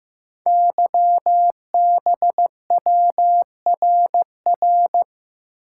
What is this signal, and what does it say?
Morse code, heard as a keyed tone, about 15 wpm, 700 Hz. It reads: YBWRR
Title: Code Groups - 5 Letters - 15wpm